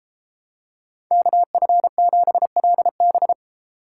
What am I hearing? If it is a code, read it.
Morse code, heard as a keyed tone, about 31 wpm, 700 Hz. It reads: KF7LB